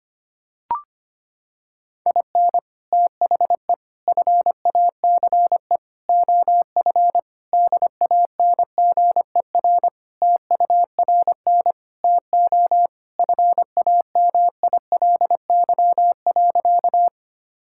Morse code, heard as a keyed tone, about 25 wpm, 700 Hz. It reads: IN THE FACE OF DANGER TURN TO FAMILY.
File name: practice / 25wpm / Sentences with 3-Word Phrases - 25wpm